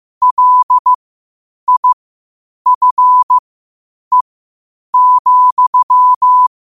Morse code, heard as a keyed tone, about 15 wpm, spaced as for 8 wpm, 1000 Hz. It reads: LIFE,
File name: qotdfiles / 8wpm